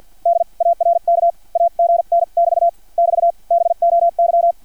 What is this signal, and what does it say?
Morse code, heard as a keyed tone, about 50 wpm, spaced as for 42 wpm, 680 Hz. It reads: DARK AGN <BT> <BT> BOY
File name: archived / CRSnet